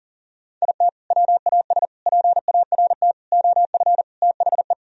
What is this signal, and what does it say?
Morse code, heard as a keyed tone, about 40 wpm, 700 Hz. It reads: IT WAS PART OF THE
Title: Sentences from Top 1000 Words - Review Part 4 - 40wpm